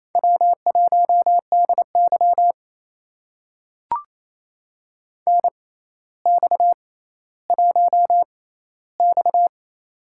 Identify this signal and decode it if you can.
Morse code, heard as a keyed tone, about 27 wpm, spaced as for 14 wpm, 700 Hz. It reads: W1DY NX1X
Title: USA Call Signs - 4 Characters - 6x Character Spacing - 28wpm